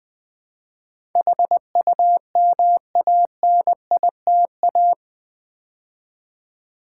Morse code, heard as a keyed tone, about 20 wpm, 700 Hz. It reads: HUMANITA
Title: Build A Word - 4 Letters At A Time - 12-Letter Words - 20wpm